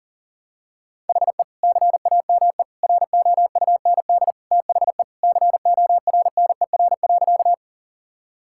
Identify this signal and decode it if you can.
Morse code, heard as a keyed tone, about 40 wpm, 700 Hz. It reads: HE CAME ROUND THE CORNER.
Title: Sentences from Top 1600 Words 40wpm